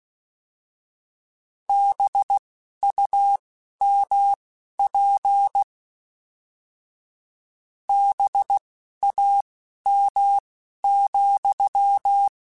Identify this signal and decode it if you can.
Morse code, heard as a keyed tone, about 16 wpm, spaced as for 11 wpm, 800 Hz. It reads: BUMP BAM,